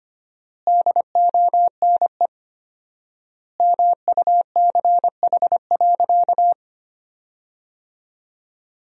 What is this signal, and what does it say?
Morse code, heard as a keyed tone, about 25 wpm, 700 Hz. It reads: DONE MUCH.
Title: 4x Word-Spacing - Sentences from Top 500 Words - 25wpm